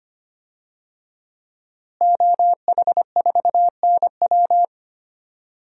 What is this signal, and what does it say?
Morse code, heard as a keyed tone, about 25 wpm, 700 Hz. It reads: OH4NW
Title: DXCC-100 Call Signs - 5 Characters - Repeated 3x - 25wpm